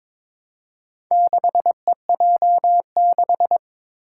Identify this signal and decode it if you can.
Morse code, heard as a keyed tone, about 23 wpm, 700 Hz. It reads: 6EJ6
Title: Four Challenging Letter-Number Combinations 22wpm